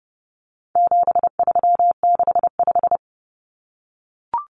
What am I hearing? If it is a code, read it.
Morse code, heard as a keyed tone, about 30 wpm, 700 Hz. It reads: 7365
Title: Common Cut Numbers - 4-Digit 30wpm